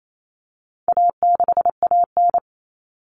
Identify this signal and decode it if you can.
Morse code, heard as a keyed tone, about 28 wpm, 700 Hz. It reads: A6AN
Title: Common Cut Numbers - 4-Digit 28wpm